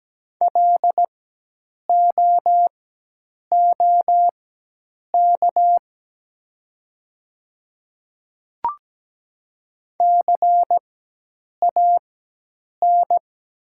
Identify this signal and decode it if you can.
Morse code, heard as a keyed tone, about 17 wpm, spaced as for 8 wpm, 700 Hz. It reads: LOOK CAN
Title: 4x Character Spacing - Top 200 Words - 17wpm